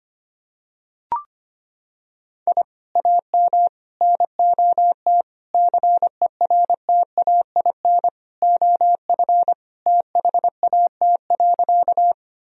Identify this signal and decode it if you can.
Morse code, heard as a keyed tone, about 25 wpm, 700 Hz. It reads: I AM NOT CERTAIN OF THAT.
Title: Sentences from Top 1000 Words - Review Part 2 - 25wpm